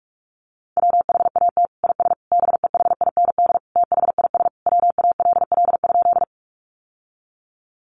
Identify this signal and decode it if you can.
Morse code, heard as a keyed tone, about 44 wpm, 700 Hz. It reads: WHAT IS BEHIND THIS WALL?
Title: Sentences from Top 1100 Words 45wpm